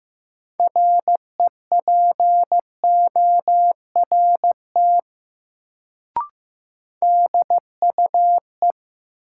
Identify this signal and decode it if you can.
Morse code, heard as a keyed tone, about 15 wpm, 700 Hz. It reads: REPORT DUE
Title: Sentences from Top 900 Words - New Word Drill - 15wpm